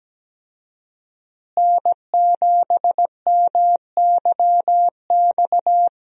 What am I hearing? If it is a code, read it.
Morse code, heard as a keyed tone, about 17 wpm, 700 Hz. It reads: N7MYX
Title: USA Call Signs - 5 Characters - Repeated 2x - 17wpm